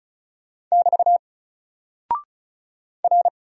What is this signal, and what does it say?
Morse code, heard as a keyed tone, about 36 wpm, 700 Hz. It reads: <BT> R
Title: Single Character (Letter:Number:Punctuation:Prosign) Alt 35wpm